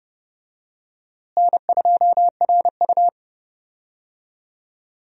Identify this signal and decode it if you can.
Morse code, heard as a keyed tone, about 30 wpm, 700 Hz. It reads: N2RU
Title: USA Call Signs - 4 Characters - Repeated 3x - 30wpm